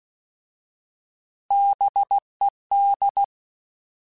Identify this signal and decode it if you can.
Morse code, heard as a keyed tone, about 16 wpm, 800 Hz. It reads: BED
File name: Alphabetical / Alphabetical mp3s / Bs @15